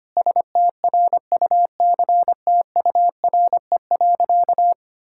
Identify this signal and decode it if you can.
Morse code, heard as a keyed tone, about 25 wpm, 700 Hz. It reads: STRUCTURE.
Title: Sentences from Top 1000 Words - Review Part 6 - 25wpm